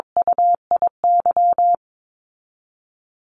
Morse code, heard as a keyed tone, about 22 wpm, 700 Hz. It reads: UIY
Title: Call Signs Repeated 3x - 22wpm